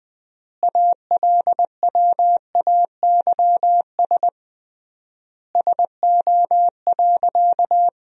Morse code, heard as a keyed tone, about 20 wpm, 700 Hz. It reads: ALWAYS SO.
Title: 3x Word-Spacing - Sentences from Top 300 Words - 20wpm